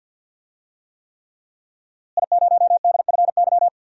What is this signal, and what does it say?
Morse code, heard as a keyed tone, about 51 wpm, 700 Hz. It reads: I0DFY
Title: DXCC-100 Call Signs - 5 Characters - Repeated 3x - 50wpm